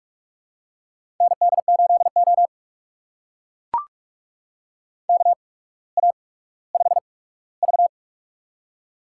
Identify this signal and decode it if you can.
Morse code, heard as a keyed tone, about 44 wpm, spaced as for 19 wpm, 700 Hz. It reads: ND8O KA5V
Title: USA Call Signs - 4 Characters - 8x Character Spacing - 45wpm